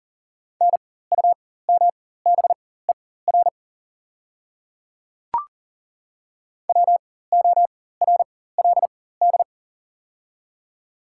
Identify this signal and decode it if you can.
Morse code, heard as a keyed tone, about 37 wpm, spaced as for 18 wpm, 700 Hz. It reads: NUMBER WORLD
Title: 4x Character Spacing - Top 200 Words - 40wpm